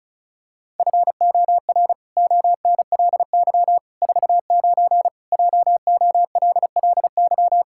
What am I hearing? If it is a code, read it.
Morse code, heard as a keyed tone, about 34 wpm, 700 Hz. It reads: FOR ONLY 49 JOLLY
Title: Pangram Challenge v2 - 35wpm